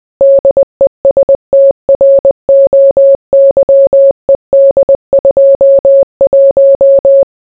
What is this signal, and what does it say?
Morse code, heard as a keyed tone, about 20 wpm, 550 Hz. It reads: DESTROYED 21